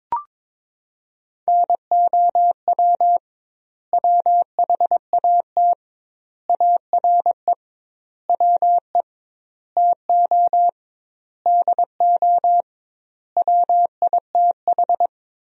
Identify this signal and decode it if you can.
Morse code, heard as a keyed tone, about 22 wpm, 700 Hz. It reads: NOW WHAT ARE WE TO DO WITH
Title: 2x Word-Spacing - Sentences from Top 100 Words - 22wpm